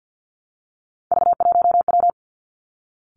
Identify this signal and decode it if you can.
Morse code, heard as a keyed tone, about 50 wpm, 700 Hz. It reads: V1W
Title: Three Challenging Letter-Number Combinations 50wpm